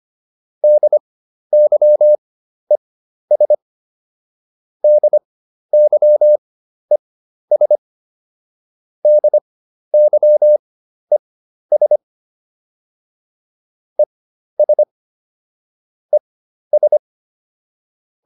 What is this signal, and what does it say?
Morse code, heard as a keyed tone, about 25 wpm, spaced as for 12 wpm, 600 Hz. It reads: DYES DYES DYES ES ES